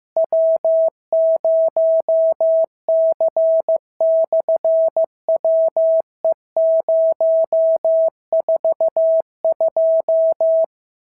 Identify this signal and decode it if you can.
Morse code, heard as a keyed tone, about 15 wpm, 650 Hz. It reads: W0C/WE042
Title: SOTA QSO Practice 15wpm